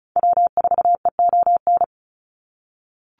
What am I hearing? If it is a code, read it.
Morse code, heard as a keyed tone, about 34 wpm, 700 Hz. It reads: W4EON
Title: NA SOTA Activator Callsigns 35wpm